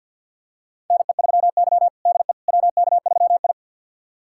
Morse code, heard as a keyed tone, about 51 wpm, 700 Hz. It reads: NE3Y DE WK3I
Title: USA Callsign DE Callsign - 4 Characters - 50wpm